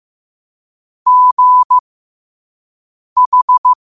Morse code, heard as a keyed tone, about 15 wpm, spaced as for 5 wpm, 1000 Hz. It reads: GH
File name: qotdfiles / 5wpm